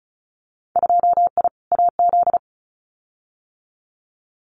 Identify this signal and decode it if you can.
Morse code, heard as a keyed tone, about 34 wpm, 700 Hz. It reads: 2I AZ
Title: Winter Field Day Exchanges 35wpm